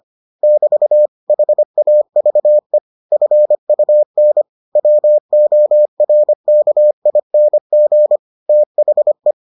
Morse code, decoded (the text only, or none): <BT> HAVE FUN WORKING THE